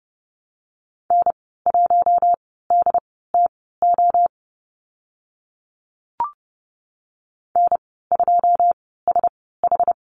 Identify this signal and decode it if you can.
Morse code, heard as a keyed tone, about 30 wpm, spaced as for 17 wpm, 700 Hz. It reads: N1DTO N2SH
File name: Lesson 14g - Number 2 - Call Signs 30wpm